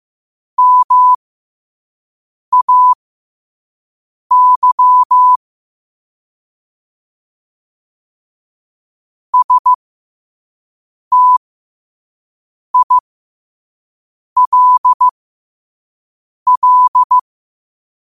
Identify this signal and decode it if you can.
Morse code, heard as a keyed tone, about 15 wpm, spaced as for 5 wpm, 1000 Hz. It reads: MAY STILL
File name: qotdfiles / 5wpm